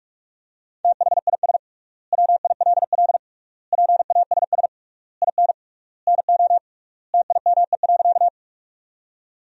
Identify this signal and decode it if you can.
Morse code, heard as a keyed tone, about 46 wpm, 700 Hz. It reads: THIS WILL PASS IN NO TIME.
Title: 3x Word-Spacing - Sentences from Top 400 Words - 45wpm